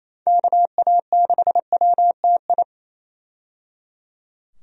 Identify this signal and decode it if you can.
Morse code, heard as a keyed tone, about 28 wpm, 700 Hz. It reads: KA6WTI
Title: USA Call Signs - 6 Characters - 28wpm